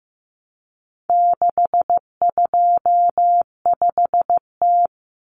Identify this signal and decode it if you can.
Morse code, heard as a keyed tone, about 15 wpm, 700 Hz. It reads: 625T
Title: Common Cut Numbers - 4-Digit 15wpm